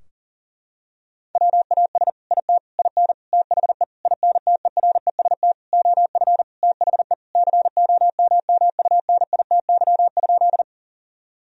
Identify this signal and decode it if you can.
Morse code, heard as a keyed tone, about 37 wpm, 700 Hz. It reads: WAS IT IN THE INTEREST OF THE COMMUNITY?
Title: Sentences from Top 1000 Words - Review Part 4 - 40wpm